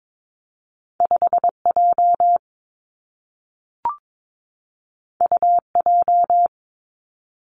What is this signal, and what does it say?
Morse code, heard as a keyed tone, about 22 wpm, 700 Hz. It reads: 5 J U J
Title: Two Challenging Letter-Number Combinations 22wpm